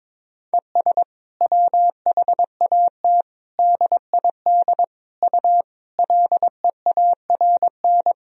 Morse code, heard as a keyed tone, about 22 wpm, 700 Hz. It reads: ES WHAT DID U LEARN
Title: Sentences from Top 300 Words and Abbreviations 22wpm